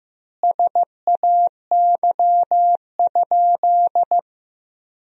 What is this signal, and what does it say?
Morse code, heard as a keyed tone, about 15 wpm, 700 Hz. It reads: SAY?
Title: 3x Word-Spacing - Sentences from Top 500 Words - 15wpm